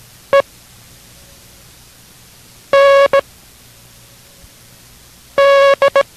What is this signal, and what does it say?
Morse code, heard as a keyed tone, about 17 wpm, spaced as for 4 wpm, 560 Hz. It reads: END